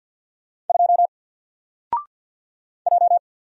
Morse code, heard as a keyed tone, about 48 wpm, 700 Hz. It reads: 2 J
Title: Single Letter-Number Alt 50wpm